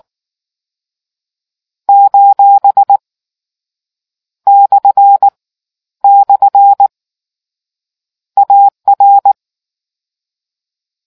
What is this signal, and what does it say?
Morse code, heard as a keyed tone, about 18 wpm, 800 Hz. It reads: : / / AR